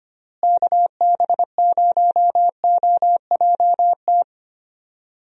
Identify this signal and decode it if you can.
Morse code, heard as a keyed tone, about 25 wpm, 700 Hz. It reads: KB0OJT
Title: USA Call Signs - 6 Characters - Repeated 2x - 25wpm